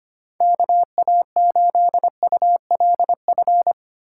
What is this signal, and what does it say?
Morse code, heard as a keyed tone, about 25 wpm, 700 Hz. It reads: KA8ULF
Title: USA Call Signs - 6 Characters - 2x Character Spacing - 25wpm